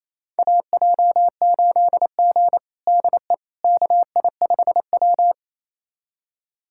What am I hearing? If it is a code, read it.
Morse code, heard as a keyed tone, about 28 wpm, 700 Hz. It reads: AJ8G DE KI5W
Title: USA Callsign DE Callsign - 4 Characters - 28wpm